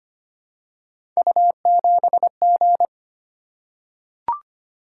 Morse code, heard as a keyed tone, about 25 wpm, 700 Hz. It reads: U7G
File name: Three Letter-Number Combinations - 25wpm